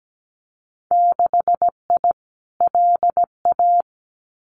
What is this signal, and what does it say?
Morse code, heard as a keyed tone, about 17 wpm, 700 Hz. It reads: 6I LA
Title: Winter Field Day Exchanges Encore 17wpm